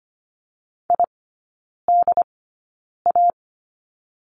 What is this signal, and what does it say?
Morse code, heard as a keyed tone, about 25 wpm, spaced as for 9 wpm, 700 Hz. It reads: IDA
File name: Lesson 17e - Letter u - Three Characters 25wpm